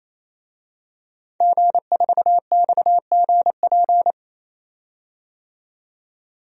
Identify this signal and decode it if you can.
Morse code, heard as a keyed tone, about 28 wpm, 700 Hz. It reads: G4XGP
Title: DXCC-100 Call Signs - 5 Characters - 28wpm